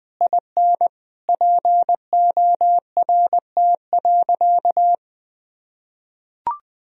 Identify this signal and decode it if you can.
Morse code, heard as a keyed tone, about 20 wpm, 700 Hz. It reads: IN PORT.
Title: Sentences from Top 1000 Words - Review Part 1 - 20wpm